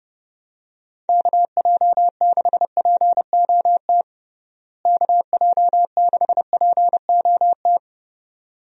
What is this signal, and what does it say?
Morse code, heard as a keyed tone, about 32 wpm, 700 Hz. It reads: KJ6POT KJ6POT
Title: USA Call Signs - 6 Characters - Repeated 2x - 30wpm